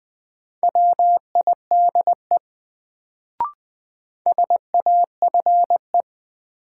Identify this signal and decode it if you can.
Morse code, heard as a keyed tone, about 20 wpm, 700 Hz. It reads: WIDE SAFE
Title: Sentences from Top 900 Words - New Word Drill - 20wpm